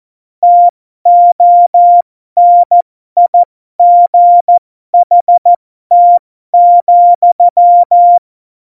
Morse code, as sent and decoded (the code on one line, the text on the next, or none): - --- -. .. --. .... - --..--
TONIGHT,